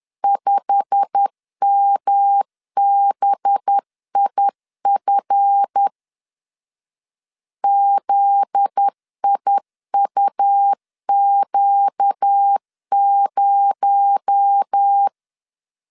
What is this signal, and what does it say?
Morse code, heard as a keyed tone, about 10 wpm, 800 Hz. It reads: 5MBIF ZIUQ0
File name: LEZIONI / 8 YJ45 / 8 YJ45 10wpm 2549